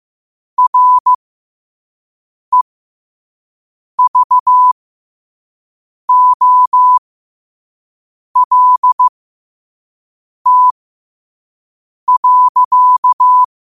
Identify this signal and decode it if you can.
Morse code, heard as a keyed tone, about 15 wpm, spaced as for 5 wpm, 1000 Hz. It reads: REVOLT.
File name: qotdfiles / 5wpm